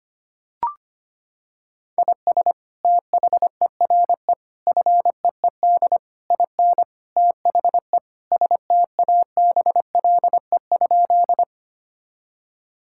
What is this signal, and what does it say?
Morse code, heard as a keyed tone, about 25 wpm, 700 Hz. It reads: IS THERE FEED IN THE STABLE?
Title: Sentences from Top 1600 Words 25wpm